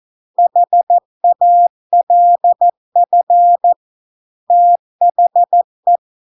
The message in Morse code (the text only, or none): HALF THE